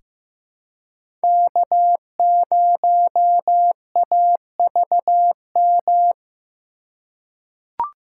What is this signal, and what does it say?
Morse code, heard as a keyed tone, about 15 wpm, 700 Hz. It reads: K0AVM
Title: USA Call Signs - 5 Characters - 2x Character Spacing - 15wpm